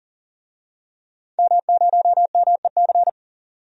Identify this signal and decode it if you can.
Morse code, heard as a keyed tone, about 37 wpm, 700 Hz. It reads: M0MEC
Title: DXCC-100 Call Signs - 5 Characters - 40wpm